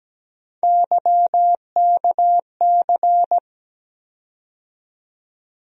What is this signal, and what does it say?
Morse code, heard as a keyed tone, about 17 wpm, 700 Hz. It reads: YKC
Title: USA Call Sign Postfixes - 3 Letters 17wpm